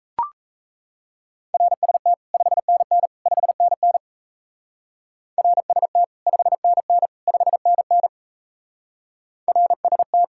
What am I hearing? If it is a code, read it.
Morse code, heard as a keyed tone, about 34 wpm, 700 Hz. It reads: RST 5NN 5NN RST 5NN 5NN RST
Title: Speed-Racing Common QSO Phrases - 28wpm